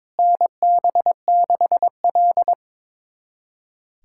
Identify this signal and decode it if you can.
Morse code, heard as a keyed tone, about 21 wpm, 700 Hz. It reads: NB6L
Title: USA Callsign DE Callsign - 4 Characters - 22wpm